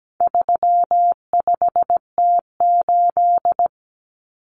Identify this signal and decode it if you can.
Morse code, heard as a keyed tone, about 17 wpm, 700 Hz. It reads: 35T8
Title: Common Cut Numbers - 4-Digit 17wpm